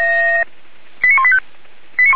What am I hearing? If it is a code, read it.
Five-tone selective call (ZVEI), heard as quick runs of five tones, tones of about 80 ms. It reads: 89185
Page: /464/SDR/ZVEI/89131/